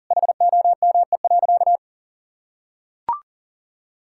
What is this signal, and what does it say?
Morse code, heard as a keyed tone, about 37 wpm, 700 Hz. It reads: HOME.
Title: Sentences from Top 1000 Words - Review Part 6 - 40wpm